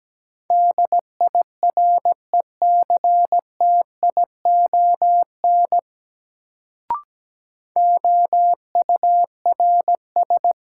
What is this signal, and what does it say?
Morse code, heard as a keyed tone, about 17 wpm, 700 Hz. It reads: DIRECTION OURS
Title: Sentences from Top 1000 Words - Review Part 5 - New Word Drill - 17wpm